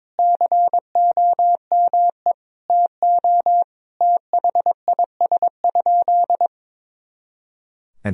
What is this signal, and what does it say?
Morse code, heard as a keyed tone, about 22 wpm, 700 Hz. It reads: COME TO THIS?
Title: Sentences from Top 100 Words and Abbreviations 22wpm